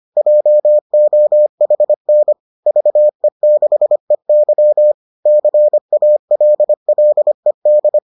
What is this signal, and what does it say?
Morse code, heard as a keyed tone, about 25 wpm, 600 Hz. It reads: JOHN VE6EY CALLED